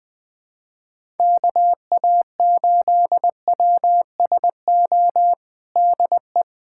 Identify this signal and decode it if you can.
Morse code, heard as a keyed tone, about 20 wpm, 700 Hz. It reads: KA8WSO DE
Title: USA Callsign DE Callsign - 6 Characters - 20wpm